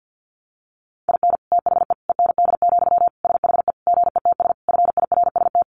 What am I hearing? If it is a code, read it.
Morse code, heard as a keyed tone, about 50 wpm, 700 Hz. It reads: IN THE END, SHE GETS FIRST
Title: Additional Easy Phrases 50wpm